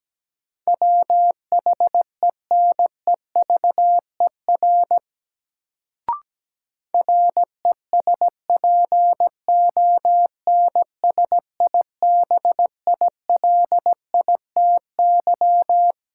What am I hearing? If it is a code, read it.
Morse code, heard as a keyed tone, about 17 wpm, 700 Hz. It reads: WHENEVER RESPONSIBILITY
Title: Sentences from Top 1200 Words - New Word Drill Alt - 17wpm